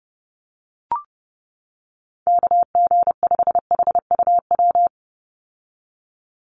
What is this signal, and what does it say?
Morse code, heard as a keyed tone, about 30 wpm, 700 Hz. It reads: KG5HUW
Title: USA Call Signs 30wpm